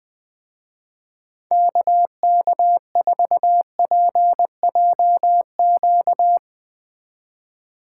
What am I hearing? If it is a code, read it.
Morse code, heard as a keyed tone, about 20 wpm, 700 Hz. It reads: KK4PJQ
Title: Speed Racing - USA 2x3 Call Signs 17wpm